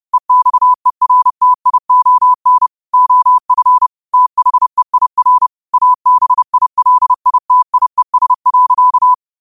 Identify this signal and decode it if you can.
Morse code, heard as a keyed tone, about 30 wpm, 1000 Hz. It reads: EXERTION OF THEIR ABILITIES.
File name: qotdfiles / 30wpm